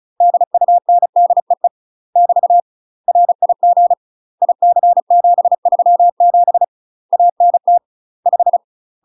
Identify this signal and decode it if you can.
Morse code, heard as a keyed tone, about 36 wpm, 700 Hz. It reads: DUNDEE <BT> RIG IC737 ANT 5
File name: UK MASTER QSOs TRAINING / 35WPM / UK MASTER QSOs 0033